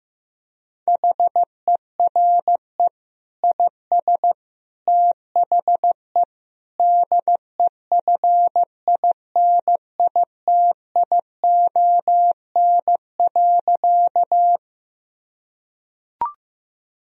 Morse code, heard as a keyed tone, about 15 wpm, 700 Hz. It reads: HERE IS THE DEFINITION.
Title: Sentences from Top 1000 Words - Review Part 4 - 15wpm